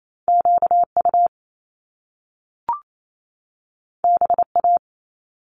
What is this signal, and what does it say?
Morse code, heard as a keyed tone, about 28 wpm, 700 Hz. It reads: QU BA
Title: Top Two Letter Combinations in Words 28wpm